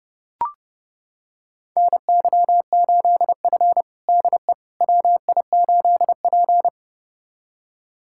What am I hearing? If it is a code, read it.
Morse code, heard as a keyed tone, about 32 wpm, 700 Hz. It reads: NY8F DE WI8P
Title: USA Callsign DE Callsign - 4 Characters - 30wpm